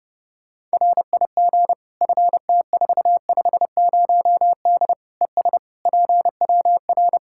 Morse code, heard as a keyed tone, about 30 wpm, 700 Hz. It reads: RIG FT450D ES PWR
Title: Practice QSO - Exchange 2 - Info - 30wpm